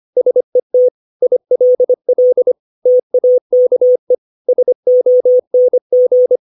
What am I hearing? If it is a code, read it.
Morse code, heard as a keyed tone, about 25 wpm, 500 Hz. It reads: SET ILL TAKE SONG
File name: 632 words / 632 Words 25WPM